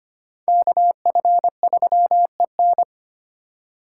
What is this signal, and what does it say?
Morse code, heard as a keyed tone, about 32 wpm, 700 Hz. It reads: KF3EN